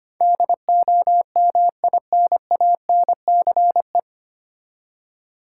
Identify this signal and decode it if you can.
Morse code, heard as a keyed tone, about 25 wpm, 700 Hz. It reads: DOMINANCE
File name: Nine Letter Words - 25wpm